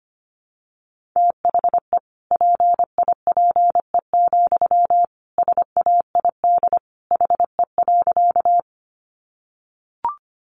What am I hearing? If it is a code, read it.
Morse code, heard as a keyed tone, about 25 wpm, 700 Hz. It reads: THE PIPE, SAID HE.
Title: Sentences from Top 1500 Words 25wpm